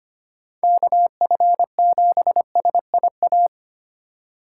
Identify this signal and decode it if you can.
Morse code, heard as a keyed tone, about 30 wpm, 700 Hz. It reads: KF7SIA
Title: Speed Racing - USA 2x3 Call Signs 25wpm